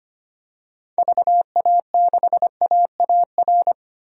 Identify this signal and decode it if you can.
Morse code, heard as a keyed tone, about 25 wpm, 700 Hz. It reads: VA6AAR
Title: DXCC-100 Call Signs - 6 Characters - 25wpm